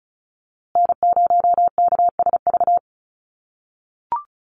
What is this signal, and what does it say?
Morse code, heard as a keyed tone, about 34 wpm, 700 Hz. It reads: N0KSV
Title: USA Call Signs Repeated 2x 35wpm